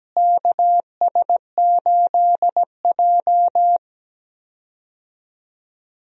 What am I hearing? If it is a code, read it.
Morse code, heard as a keyed tone, about 17 wpm, 700 Hz. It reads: KS8J
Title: USA Call Signs - 4 Characters - Repeated 3x - 17wpm